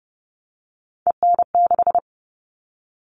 Morse code, heard as a keyed tone, about 30 wpm, 700 Hz. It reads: EN6
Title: All Cut Numbers - 3-Digit 30wpm